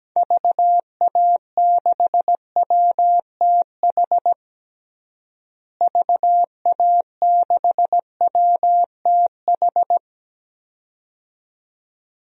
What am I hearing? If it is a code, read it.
Morse code, heard as a keyed tone, about 17 wpm, 700 Hz. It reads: VA6WTH VA6WTH
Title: DXCC-100 Call Signs - 6 Characters - Repeated 2x - 17wpm